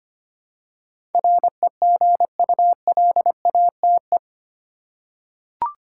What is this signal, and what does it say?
Morse code, heard as a keyed tone, about 25 wpm, 700 Hz. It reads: REGULATE
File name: 8 Letter Words - First 3 Letters Given - 25wpm